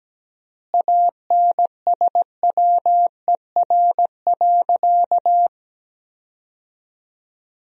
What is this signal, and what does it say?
Morse code, heard as a keyed tone, about 17 wpm, 700 Hz. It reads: ANSWER.
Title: 3x Word-Spacing - Sentences from Top 300 Words - 17wpm